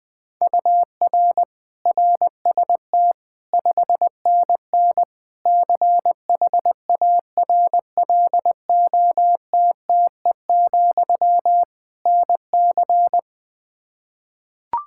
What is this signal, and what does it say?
Morse code, heard as a keyed tone, about 20 wpm, 700 Hz. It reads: UR RST 5NN CHARLOTTE, NC
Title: K1USN SST QSO Practice 20wpm